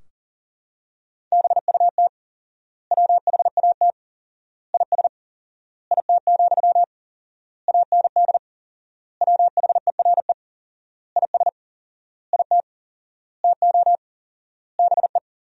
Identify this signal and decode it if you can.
Morse code, heard as a keyed tone, about 40 wpm, 700 Hz. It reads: BUT WHAT IS IT, AND WHERE IS IT TO BE
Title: 4x Word-Spacing - Sentences from Top 300 Words - 40wpm